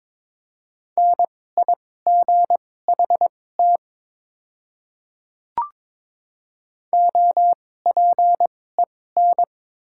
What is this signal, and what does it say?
Morse code, heard as a keyed tone, about 22 wpm, spaced as for 16 wpm, 700 Hz. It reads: NIGHT OPEN
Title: 2x Character Spacing - Top 300 Words - 22wpm